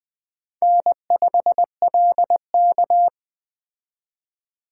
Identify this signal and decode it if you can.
Morse code, heard as a keyed tone, about 20 wpm, 700 Hz. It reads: N5LK
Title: Speed Racing - USA 1x2 Call Signs 17wpm